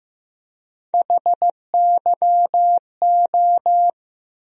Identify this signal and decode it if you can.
Morse code, heard as a keyed tone, about 15 wpm, 700 Hz. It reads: HYO
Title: USA Call Sign Postfixes - 3 Letters 15wpm